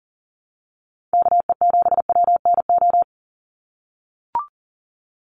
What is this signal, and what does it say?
Morse code, heard as a keyed tone, about 40 wpm, 700 Hz. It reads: KE7WNO
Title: USA Call Signs 40wpm